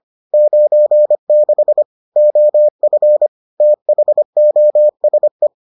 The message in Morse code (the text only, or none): 96 OF THOSE